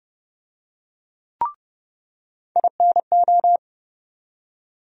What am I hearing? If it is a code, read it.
Morse code, heard as a keyed tone, about 30 wpm, 700 Hz. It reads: INO
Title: USA Call Sign Postfixes - 3 Letters 30wpm